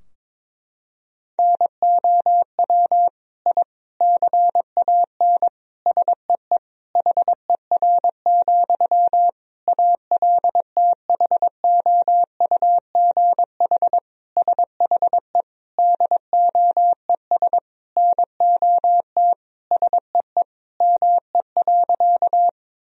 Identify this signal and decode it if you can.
Morse code, heard as a keyed tone, about 22 wpm, 700 Hz. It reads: NOW I CAN SEE HER, ALTHOUGH SHE DOES NOT SEE ME.
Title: Sentences from Top 1000 Words - Review Part 3 - 22wpm